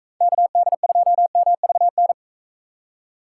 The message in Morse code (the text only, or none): KD2MVN